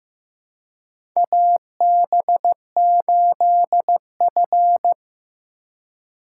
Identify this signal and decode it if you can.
Morse code, heard as a keyed tone, about 15 wpm, 700 Hz. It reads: AB8F
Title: USA Call Signs - 4x Character Spacing - 15wpm